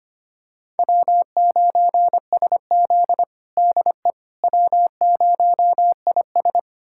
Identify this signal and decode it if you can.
Morse code, heard as a keyed tone, about 25 wpm, 700 Hz. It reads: W9SZ DE W0IS
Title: USA Callsign DE Callsign - 4 Characters - 25wpm